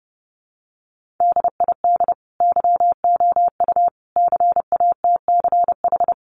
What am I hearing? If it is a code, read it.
Morse code, heard as a keyed tone, about 30 wpm, 700 Hz. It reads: DID YOU CATCH
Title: Sentences from Top 1500 Words 30wpm